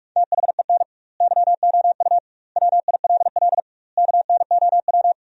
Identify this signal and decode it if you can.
Morse code, heard as a keyed tone, about 41 wpm, 700 Hz. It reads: THEN YOU WILL KNOW
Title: 2x Word-Spacing - Sentences from Top 400 Words - 45wpm